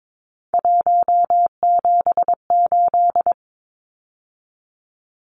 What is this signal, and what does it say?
Morse code, heard as a keyed tone, about 22 wpm, 700 Hz. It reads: 178
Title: Numbers - 3-Digit 22wpm